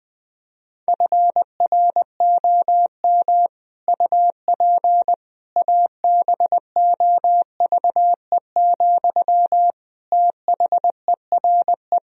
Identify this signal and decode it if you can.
Morse code, heard as a keyed tone, about 20 wpm, 700 Hz. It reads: FROM UP ABOVE, THERE
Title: Sentences from Top 1000 Words - Review Part 2 - 20wpm